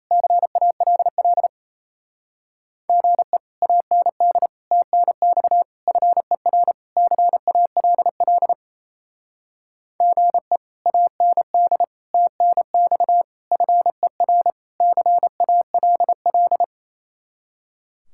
Morse code, heard as a keyed tone, about 34 wpm, 700 Hz. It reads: CALL GE AND TNX FER CALL GE AND TNX FER CALL
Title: Speed-Racing Common QSO Phrases Encore - 28wpm